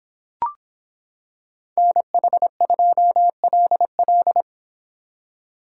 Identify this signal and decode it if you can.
Morse code, heard as a keyed tone, about 25 wpm, 700 Hz. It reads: NH2LL